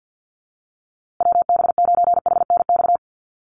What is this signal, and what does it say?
Morse code, heard as a keyed tone, about 50 wpm, 700 Hz. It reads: WB9HNX
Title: USA Call Signs Repeated 3x 50wpm